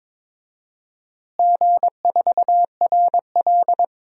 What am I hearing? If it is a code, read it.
Morse code, heard as a keyed tone, about 22 wpm, 700 Hz. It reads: G4RL
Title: DXCC-100 Call Signs - 4 Characters - Repeated 3x - 22wpm